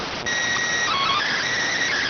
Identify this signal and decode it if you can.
SSTV (slow-scan television), PD 290.